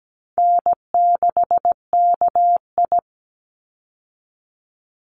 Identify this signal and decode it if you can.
Morse code, heard as a keyed tone, about 17 wpm, 700 Hz. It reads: N6KI
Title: Easy Call Signs Repeated 3x - 17wpm